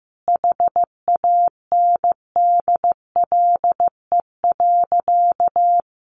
Morse code, heard as a keyed tone, about 15 wpm, 700 Hz. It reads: HANDLE.
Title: Sentences from Top 1100 Words 15wpm